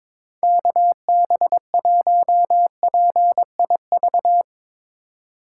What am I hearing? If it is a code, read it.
Morse code, heard as a keyed tone, about 22 wpm, 700 Hz. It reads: KB1PIV
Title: USA Call Signs - 6 Characters - 4x Character Spacing - 22wpm